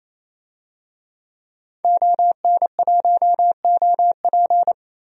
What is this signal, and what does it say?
Morse code, heard as a keyed tone, about 28 wpm, 700 Hz. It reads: ON1OP
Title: DXCC-100 Call Signs - 5 Characters - 28wpm